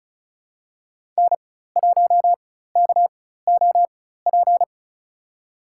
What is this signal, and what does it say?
Morse code, heard as a keyed tone, about 36 wpm, spaced as for 16 wpm, 700 Hz. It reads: N1KOP
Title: USA Call Signs - 5 Characters - 4x Character Spacing - 35wpm